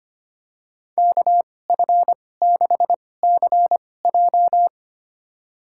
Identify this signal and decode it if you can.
Morse code, heard as a keyed tone, about 25 wpm, spaced as for 18 wpm, 700 Hz. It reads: KF6CJ
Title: USA Call Signs - 5 Characters - 2x Character Spacing - 25wpm